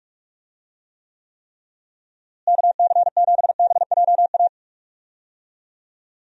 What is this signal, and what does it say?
Morse code, heard as a keyed tone, about 46 wpm, 700 Hz. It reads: KC7BJA
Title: USA Call Signs - 6 Characters - Repeated 3x - 45wpm